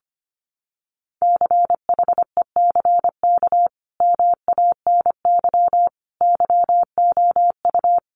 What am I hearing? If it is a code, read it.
Morse code, heard as a keyed tone, about 25 wpm, 700 Hz. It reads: CHECK MANY YOU
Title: Sentences from Top 1500 Words 25wpm